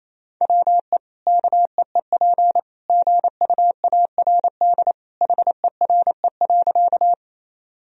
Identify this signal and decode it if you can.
Morse code, heard as a keyed tone, about 28 wpm, 700 Hz. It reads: WE KEEP GUARD HERE.
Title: Sentences from Top 1000 Words - Review Part 4 - 28wpm